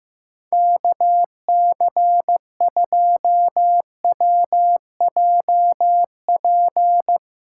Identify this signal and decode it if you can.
Morse code, heard as a keyed tone, about 15 wpm, 700 Hz. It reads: KC2WJP
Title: USA Callsign DE Callsign - 6 Characters - 15wpm